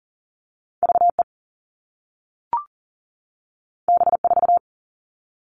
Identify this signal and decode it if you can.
Morse code, heard as a keyed tone, about 38 wpm, 700 Hz. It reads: VE B4
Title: All Cut Numbers - 2-Digit 40wpm